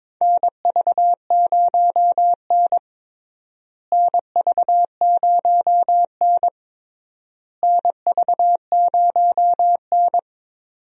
Morse code, heard as a keyed tone, about 22 wpm, 700 Hz. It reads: NV0N NV0N NV0N
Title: USA Call Signs - 4 Characters - Repeated 3x - 22wpm